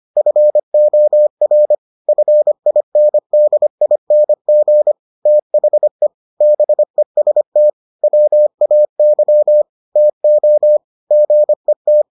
Morse code, heard as a keyed tone, about 25 wpm, 600 Hz. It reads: FOR FINDING THE BEST WAY TO GET